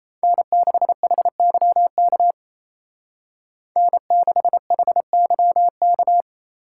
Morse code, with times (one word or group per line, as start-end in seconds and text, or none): 0.2-2.3 N6HYK
3.8-6.2 N6HYK